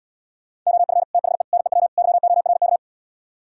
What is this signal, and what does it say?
Morse code, heard as a keyed tone, about 77 wpm, 700 Hz. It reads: OF THE NEW YORK